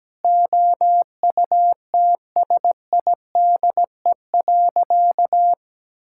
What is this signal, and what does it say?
Morse code, heard as a keyed tone, about 17 wpm, 700 Hz. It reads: OUTSIDE.
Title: Sentences from Top 1000 Words - Review Part 6 - 17wpm